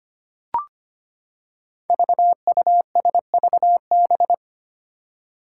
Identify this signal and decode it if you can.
Morse code, heard as a keyed tone, about 25 wpm, 700 Hz. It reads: VUSVB
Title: Five Challenging Letter-Number Combinations 25wpm